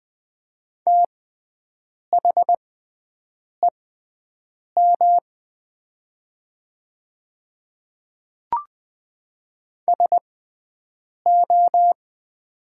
Morse code, heard as a keyed tone, about 20 wpm, spaced as for 7 wpm, 700 Hz. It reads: THEM SO
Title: 6x Character Spacing - Top 100 Words - 20wpm